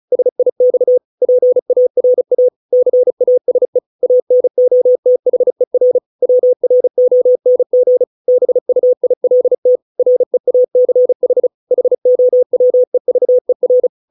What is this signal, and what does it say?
Morse code, heard as a keyed tone, about 34 wpm, 500 Hz. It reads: SIX PARA CASE ANOTHER WRONG BUILT REACH HOWEVER
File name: 632 words / 632 Words 35WPM